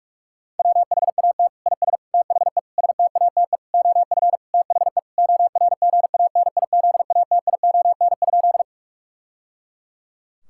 Morse code, heard as a keyed tone, about 41 wpm, 700 Hz. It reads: WHAT IS THE STATE OF THE ORGANIZATION?
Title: Sentences from Top 1000 Words - Review Part 5 - 45wpm